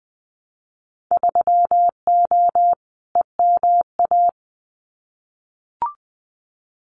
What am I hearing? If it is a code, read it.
Morse code, heard as a keyed tone, about 20 wpm, 700 Hz. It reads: 3O EMA
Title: Winter Field Day Exchanges Encore 20wpm